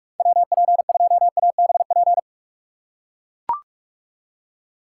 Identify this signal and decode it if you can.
Morse code, heard as a keyed tone, about 44 wpm, 700 Hz. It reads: WP2ABP
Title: USA Call Signs - 6 Characters - 4x Character Spacing - 45wpm